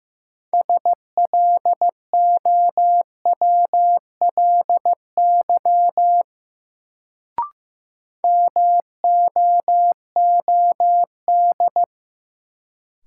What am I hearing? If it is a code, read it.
Morse code, heard as a keyed tone, about 15 wpm, 700 Hz. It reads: SLOWLY MOOD
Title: Sentences from Top 1100 Words - New Word Drill - 15wpm